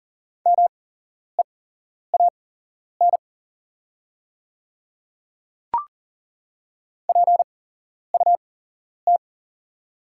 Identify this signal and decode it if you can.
Morse code, heard as a keyed tone, about 40 wpm, spaced as for 11 wpm, 700 Hz. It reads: MEAN PUT
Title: 8x Character Spacing - Top 200 Words - 40wpm